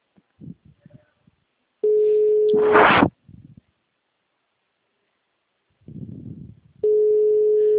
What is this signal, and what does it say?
Ringback tone (Europe, 425 Hz).
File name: out- -2001- - - .252